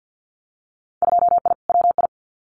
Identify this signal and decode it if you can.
Morse code, heard as a keyed tone, about 50 wpm, 700 Hz. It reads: 2I WI
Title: Winter Field Day Exchanges 50wpm